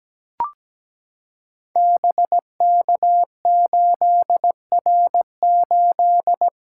Morse code, heard as a keyed tone, about 17 wpm, 700 Hz. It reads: BK8R8
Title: Code Groups - 5 Letter:Numbers - 17wpm